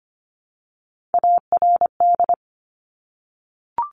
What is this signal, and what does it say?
Morse code, heard as a keyed tone, about 25 wpm, 700 Hz. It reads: ARD
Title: Top Three Letter Combinations in Words 25wpm